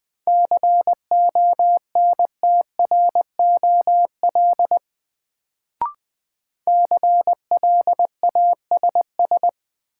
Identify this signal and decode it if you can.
Morse code, heard as a keyed tone, about 20 wpm, 700 Hz. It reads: CONTROL CLASS
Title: Sentences from Top 1000 Words - Review Part 2 - New Word Drill - 20wpm